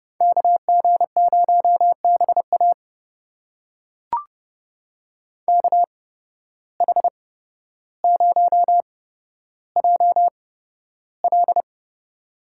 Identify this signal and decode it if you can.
Morse code, heard as a keyed tone, about 30 wpm, spaced as for 13 wpm, 700 Hz. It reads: KG0BA KH0JL
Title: USA Call Signs - 5 Characters - 8x Character Spacing - 30wpm